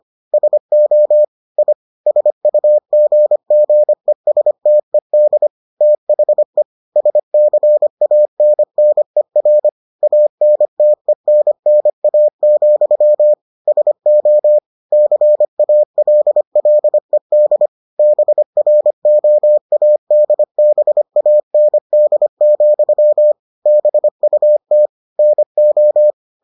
Morse code, heard as a keyed tone, about 25 wpm, 600 Hz. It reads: SO I SUGGESTED THE SCANNER ANTENNA, SO CALLED BROADBAND, BUT NO